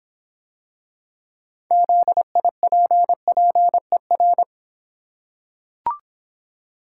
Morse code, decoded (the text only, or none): ZIPPER